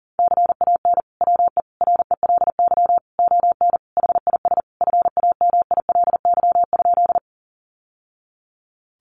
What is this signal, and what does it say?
Morse code, heard as a keyed tone, about 40 wpm, 700 Hz. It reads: CAN WE RELY ON HIS FAMILY?
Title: Sentences from Top 1200 Words 40wpm